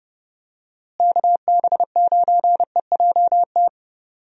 Morse code, heard as a keyed tone, about 32 wpm, 700 Hz. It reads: KB9EJT
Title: USA Call Signs - 6 Characters - Repeated 3x - 30wpm